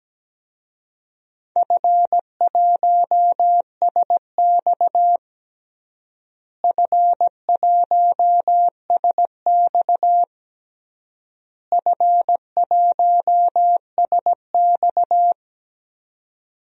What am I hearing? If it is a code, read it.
Morse code, heard as a keyed tone, about 17 wpm, 700 Hz. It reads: F1SX F1SX F1SX
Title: DXCC-100 Call Signs - 4 Characters - Repeated 3x - 17wpm